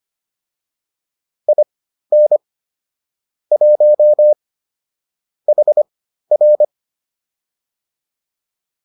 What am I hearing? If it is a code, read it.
Morse code, heard as a keyed tone, about 25 wpm, spaced as for 13 wpm, 600 Hz. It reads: IN 1 HR